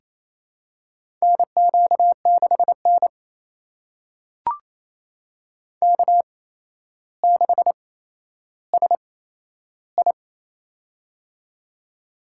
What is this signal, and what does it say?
Morse code, heard as a keyed tone, about 28 wpm, spaced as for 12 wpm, 700 Hz. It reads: NQ6N K6SI
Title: USA Call Signs - 8x Character Spacing - 28wpm